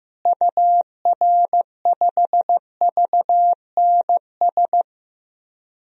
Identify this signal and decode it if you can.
Morse code, heard as a keyed tone, about 15 wpm, 700 Hz. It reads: UR5VNS
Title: DXCC-100 Call Signs - 6 Characters - Repeated 2x - 15wpm